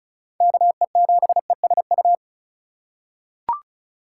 Morse code, heard as a keyed tone, about 36 wpm, 700 Hz. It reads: KE7ESU
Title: USA Call Signs - 6 Characters - 4x Character Spacing - 35wpm